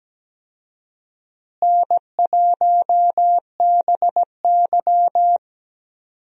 Morse code, heard as a keyed tone, about 17 wpm, 700 Hz. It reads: N1BY
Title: USA Call Signs - 4 Characters - Repeated 2x - 17wpm